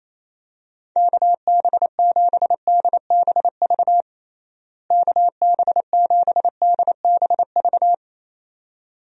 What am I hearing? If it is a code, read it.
Morse code, heard as a keyed tone, about 28 wpm, 700 Hz. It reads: KB7DBV KB7DBV
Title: USA Call Signs - 6 Characters - Repeated 2x - 28wpm